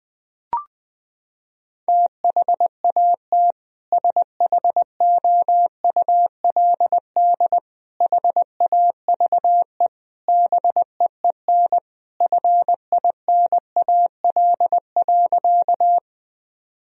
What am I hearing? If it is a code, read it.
Morse code, heard as a keyed tone, about 20 wpm, 700 Hz. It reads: THAT SHOULD HAVE BEEN FINAL.
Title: Sentences from Top 1000 Words - Review Part 2 - 20wpm